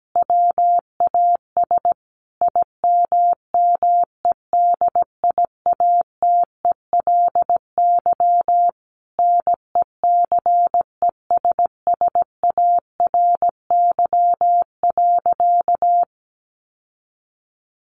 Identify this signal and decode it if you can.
Morse code, heard as a keyed tone, about 17 wpm, 700 Hz. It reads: WAS IMMEDIATELY NECESSARY.
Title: Sentences from Top 1100 Words 17wpm